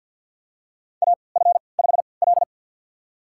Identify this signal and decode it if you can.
Morse code, heard as a keyed tone, about 53 wpm, spaced as for 29 wpm, 700 Hz. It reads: AF5L
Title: USA Call Signs - 4 Characters - 3x Character Spacing - 50wpm